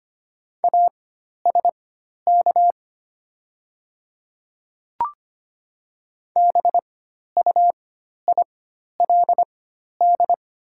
Morse code, heard as a keyed tone, about 25 wpm, spaced as for 12 wpm, 700 Hz. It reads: ASK BUILD
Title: 4x Character Spacing - Top 200 Words - 25wpm